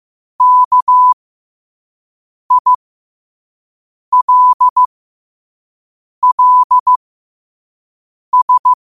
Morse code, heard as a keyed tone, about 15 wpm, spaced as for 5 wpm, 1000 Hz. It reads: KILLS
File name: qotdfiles / 5wpm